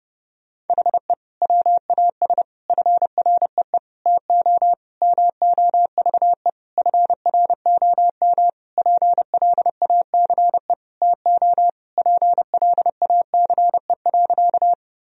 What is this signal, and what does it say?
Morse code, heard as a keyed tone, about 32 wpm, 700 Hz. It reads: HE WAS FREE TO MOVE FROM PLACE TO PLACE.
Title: Sentences from Top 1000 Words - Review Part 1 - 30wpm